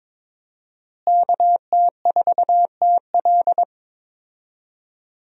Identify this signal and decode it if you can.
Morse code, heard as a keyed tone, about 22 wpm, 700 Hz. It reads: KT4TL
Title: USA Call Signs - 5 Characters - Repeated 2x - 22wpm